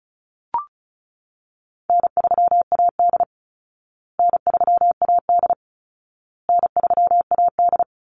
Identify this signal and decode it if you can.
Morse code, heard as a keyed tone, about 34 wpm, 700 Hz. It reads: N3AD N3AD N3AD
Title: Easy Call Signs Repeated 3x - 35wpm